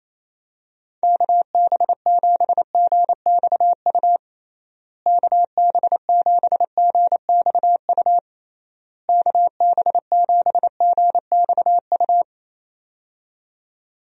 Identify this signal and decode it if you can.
Morse code, heard as a keyed tone, about 28 wpm, 700 Hz. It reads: KB7GXU KB7GXU KB7GXU
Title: USA Call Signs - 6 Characters - Repeated 3x - 28wpm